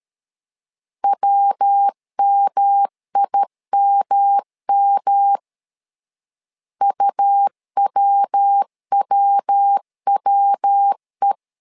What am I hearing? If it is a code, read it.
Morse code, heard as a keyed tone, about 13 wpm, 800 Hz. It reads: WMIMM UWWWE
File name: LEZIONI / 3 MWRO / 3 MWRO 12wpm 1730